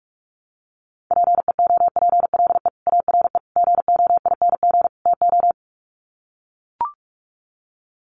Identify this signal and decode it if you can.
Morse code, heard as a keyed tone, about 44 wpm, 700 Hz. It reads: PEOPLE ARE GOING TO
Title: Sets of 4 Words 45wpm